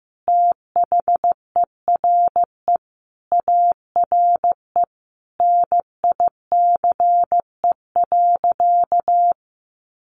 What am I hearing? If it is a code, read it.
Morse code, heard as a keyed tone, about 15 wpm, 700 Hz. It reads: THERE ARE NICE.
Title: Easy Phrases 15wpm